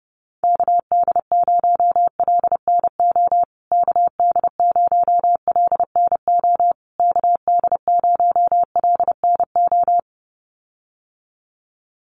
Morse code, heard as a keyed tone, about 30 wpm, 700 Hz. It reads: KD0LNO KD0LNO KD0LNO
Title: USA Call Signs Repeated 3x 30wpm